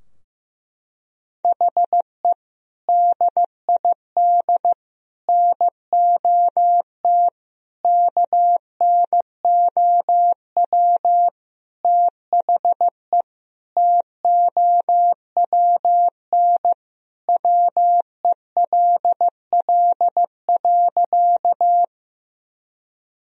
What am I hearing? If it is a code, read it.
Morse code, heard as a keyed tone, about 15 wpm, 700 Hz. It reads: HE DID NOT KNOW THE TOWN WELL.
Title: Sentences from Top 1000 Words - Review Part 2 - 15wpm